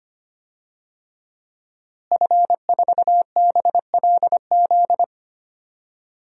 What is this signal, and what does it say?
Morse code, heard as a keyed tone, about 25 wpm, 700 Hz. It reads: F4BLZ
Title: DXCC-100 Call Signs - 5 Characters - Repeated 3x - 25wpm